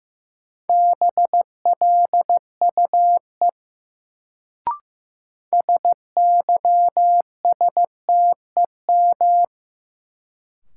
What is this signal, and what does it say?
Morse code, heard as a keyed tone, about 15 wpm, 700 Hz. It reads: BLUE SYSTEM
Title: Sentences from Top 1000 Words - Review Part 3 - New Word Drill - 15wpm